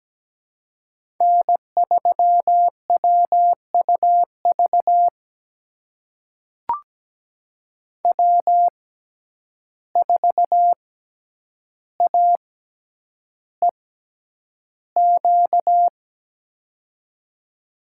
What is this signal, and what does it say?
Morse code, heard as a keyed tone, about 17 wpm, spaced as for 9 wpm, 700 Hz. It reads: N3WUV W4AEQ
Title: USA Call Signs - 5 Characters - 6x Character Spacing - 17wpm